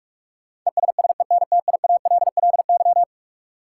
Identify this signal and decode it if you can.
Morse code, heard as a keyed tone, about 46 wpm, 700 Hz. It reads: ESSENTIALLY
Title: Sentences from Top 1300 Words - New Word Drill - 45wpm